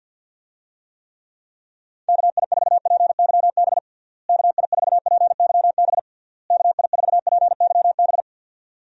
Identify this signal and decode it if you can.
Morse code, heard as a keyed tone, about 46 wpm, 700 Hz. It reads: KI4PYB KI4PYB KI4PYB
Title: USA Call Signs - 6 Characters - Repeated 3x - 50wpm